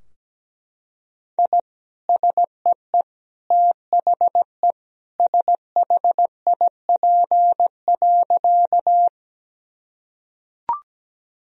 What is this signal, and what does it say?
Morse code, heard as a keyed tone, about 17 wpm, 700 Hz. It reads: I SEE THE SHIP.
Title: Sentences from Top 1000 Words - Review Part 2 - 17wpm